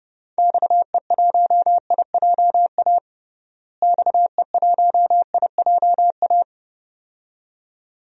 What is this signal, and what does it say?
Morse code, heard as a keyed tone, about 32 wpm, 700 Hz. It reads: XE1IJA XE1IJA
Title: DXCC-100 Call Signs - 6 Characters - Repeated 2x - 30wpm